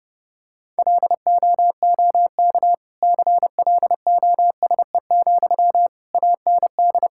Morse code, heard as a keyed tone, about 30 wpm, 700 Hz. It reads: LOOK CLOSE, AND
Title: Sentences from Top 1600 Words 30wpm